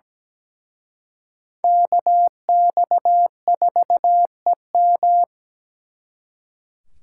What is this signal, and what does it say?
Morse code, heard as a keyed tone, about 20 wpm, 700 Hz. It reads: KX4EM
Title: Speed Racing - USA 2x2 Call Signs 17wpm